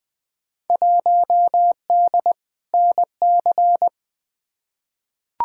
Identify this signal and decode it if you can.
Morse code, heard as a keyed tone, about 20 wpm, 700 Hz. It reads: 1D NC
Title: Field Day Exchanges - 20wpm